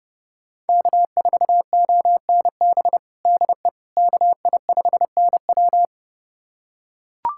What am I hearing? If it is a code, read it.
Morse code, heard as a keyed tone, about 30 wpm, 700 Hz. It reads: K4ONB DE KI5NW
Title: USA Callsign DE Callsign 30wpm